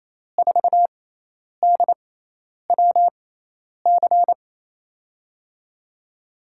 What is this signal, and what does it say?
Morse code, heard as a keyed tone, about 28 wpm, spaced as for 10 wpm, 700 Hz. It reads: 4DWC
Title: USA Call Signs - 5 Characters - 6x Character Spacing - 28wpm